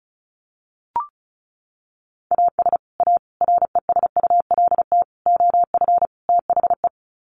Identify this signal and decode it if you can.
Morse code, heard as a keyed tone, about 34 wpm, 700 Hz. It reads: AS A RESULT OF THE
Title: Sets of 5 Words 35wpm